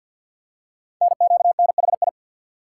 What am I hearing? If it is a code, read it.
Morse code, heard as a keyed tone, about 50 wpm, 700 Hz. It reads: NQNHI
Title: Code Groups - 5 Letters - 50wpm